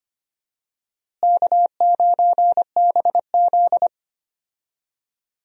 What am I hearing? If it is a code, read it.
Morse code, heard as a keyed tone, about 25 wpm, 700 Hz. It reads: K9BZ
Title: USA Call Signs - 4 Characters - Repeated 3x - 25wpm